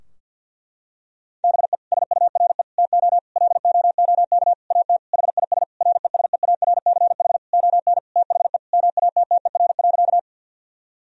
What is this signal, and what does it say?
Morse code, heard as a keyed tone, about 50 wpm, 700 Hz. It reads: BE SURE TO LOOK AT HIS RESEARCH ON THE MATTER.
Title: Sentences from Top 1000 Words - Review Part 4 - 50wpm